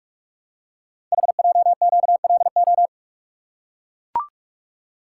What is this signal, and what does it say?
Morse code, heard as a keyed tone, about 44 wpm, 700 Hz. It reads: HJQLO
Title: Code Groups - 5 Letters - 45wpm